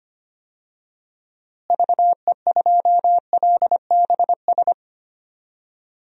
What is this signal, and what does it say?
Morse code, heard as a keyed tone, about 25 wpm, 700 Hz. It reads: VE2LBS
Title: DXCC-100 Call Signs - 6 Characters - Repeated 2x - 25wpm